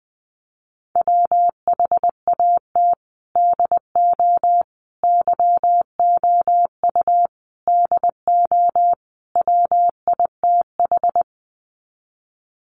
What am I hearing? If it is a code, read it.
Morse code, heard as a keyed tone, about 20 wpm, 700 Hz. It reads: WHAT DO YOU DO WITH
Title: Sets of 5 Words Encore2 20wpm